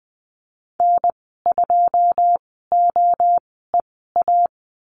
Morse code, heard as a keyed tone, about 20 wpm, spaced as for 15 wpm, 700 Hz. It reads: N2OEA
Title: 082 Lesson 14g - Number 2 - Call Signs 20wpm